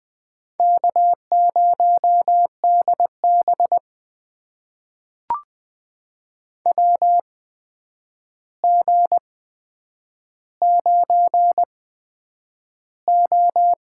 Morse code, heard as a keyed tone, about 20 wpm, spaced as for 9 wpm, 700 Hz. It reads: K0DB WG9O
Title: USA Call Signs - 4 Characters - 8x Character Spacing - 20wpm